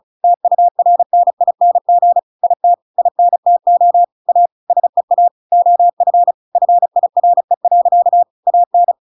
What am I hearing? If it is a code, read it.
Morse code, heard as a keyed tone, about 34 wpm, 700 Hz. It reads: TURNING IT INTO A SEA OF FIRE. AN
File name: Pinocchio / Pinocchio ch6 35 35 ik1hge